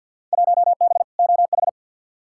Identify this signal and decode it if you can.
Morse code, heard as a keyed tone, about 52 wpm, 700 Hz. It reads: 1B OH
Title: Field Day Exchanges - 50wpm